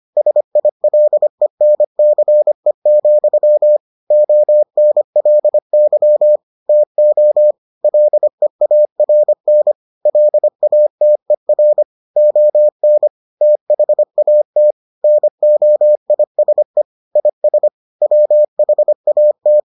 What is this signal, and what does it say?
Morse code, heard as a keyed tone, about 25 wpm, 600 Hz. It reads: SILENCE, ONLY TO LEARN LATER ON THAT NOISE IS WHAT